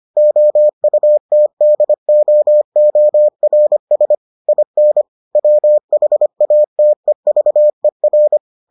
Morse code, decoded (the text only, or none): OUTDOORS IN WHATEVER